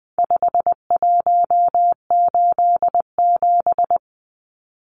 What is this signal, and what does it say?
Morse code, heard as a keyed tone, about 20 wpm, 700 Hz. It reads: 5187
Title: Numbers - 4-Digit 20wpm